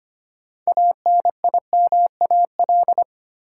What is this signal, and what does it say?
Morse code, heard as a keyed tone, about 25 wpm, 700 Hz. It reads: ANIMAL
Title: Sentences from Top 1000 Words - Review Part 1 - New Word Drill - 25wpm